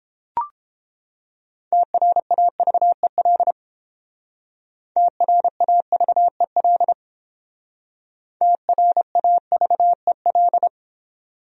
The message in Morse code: TRAVEL TRAVEL TRAVEL

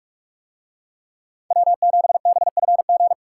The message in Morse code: W7BFG

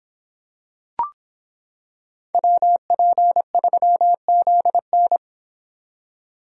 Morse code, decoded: WP3ZN

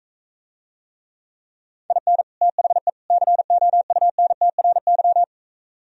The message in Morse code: IN THE COUNTRY

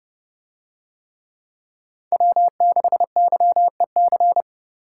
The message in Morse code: W6YEC

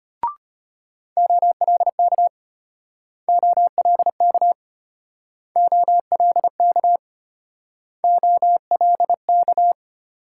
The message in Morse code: OLK OLK OLK OLK

